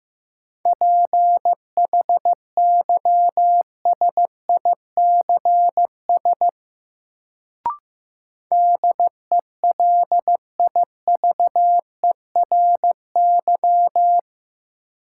Morse code, heard as a keyed tone, about 15 wpm, 700 Hz. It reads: PHYSICS DELIVERY
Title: Sentences from Top 1500 Words - New Word Drill Alt - 15wpm